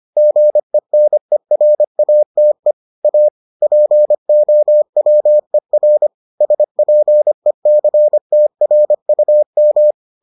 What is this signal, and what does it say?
Morse code, heard as a keyed tone, about 25 wpm, 600 Hz. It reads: GENERATE A POWER SPECTRUM